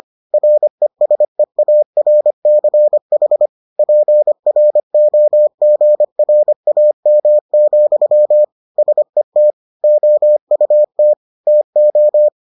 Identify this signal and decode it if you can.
Morse code, heard as a keyed tone, about 25 wpm, 600 Hz. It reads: RESEARCH PROGRAM, SET OUT TO